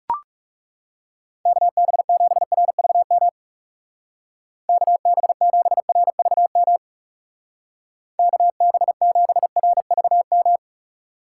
Morse code, heard as a keyed tone, about 39 wpm, 700 Hz. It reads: KB7RVM KB7RVM KB7RVM